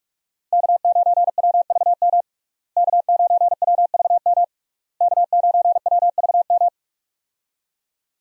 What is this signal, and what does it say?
Morse code, heard as a keyed tone, about 46 wpm, 700 Hz. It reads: K9WVM K9WVM K9WVM
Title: USA Call Signs - 5 Characters - Repeated 3x - 45wpm